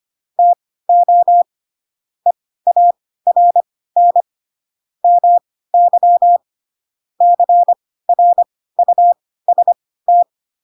Morse code, heard as a keyed tone, about 25 wpm, spaced as for 16 wpm, 700 Hz. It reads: TO EARN MY CRUST
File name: Pinocchio / Pinocchio ch2 10 25 ik1hge